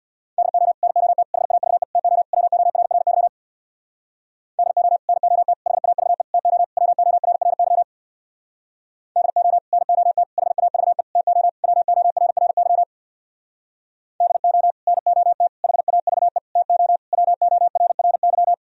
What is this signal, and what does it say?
Morse code, heard as a keyed tone, about 65 wpm, 700 Hz. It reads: DO NOT HAVE TO WORRY DO NOT HAVE TO WORRY DO NOT HAVE TO WORRY DO NOT HAVE TO WORRY